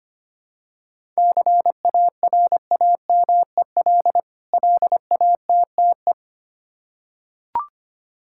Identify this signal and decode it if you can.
Morse code, heard as a keyed tone, about 25 wpm, 700 Hz. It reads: CARAMEL LATTE
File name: practice / 25wpm / Difficult Words with Hints - 25wpm